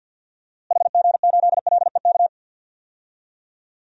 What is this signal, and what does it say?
Morse code, heard as a keyed tone, about 53 wpm, 700 Hz. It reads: HG8LEK
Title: DXCC-100 Call Signs - 6 Characters - Repeated 2x - 50wpm